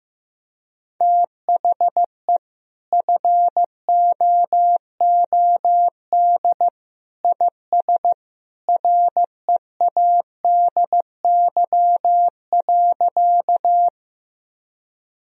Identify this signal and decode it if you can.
Morse code, heard as a keyed tone, about 15 wpm, 700 Hz. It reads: THE FOOD IS READY.
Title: Sentences from Top 1000 Words - Review Part 1 - 15wpm